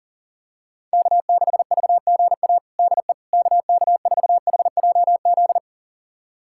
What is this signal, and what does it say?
Morse code, heard as a keyed tone, about 40 wpm, 700 Hz. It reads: K6VGA DE KK4HJZ
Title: USA Callsign DE Callsign 40wpm